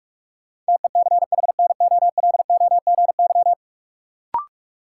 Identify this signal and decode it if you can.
Morse code, heard as a keyed tone, about 41 wpm, 700 Hz. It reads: TECHNOLOGY
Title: Sentences from Top 1200 Words - New Word Drill Alt - 45wpm